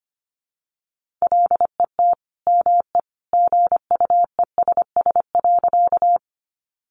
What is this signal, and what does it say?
Morse code, heard as a keyed tone, about 25 wpm, 700 Hz. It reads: LET ME GUESS.
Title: Sentences from Top 1500 Words 25wpm